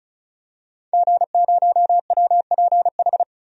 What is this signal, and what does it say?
Morse code, heard as a keyed tone, about 36 wpm, 700 Hz. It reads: G0WPH
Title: DXCC-100 Call Signs - 5 Characters - Repeated 3x - 35wpm